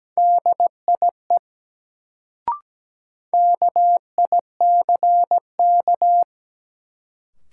Morse code, heard as a keyed tone, about 17 wpm, 700 Hz. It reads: DIE KICK
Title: Sentences from Top 1300 Words - New Word Drill Alt - 17wpm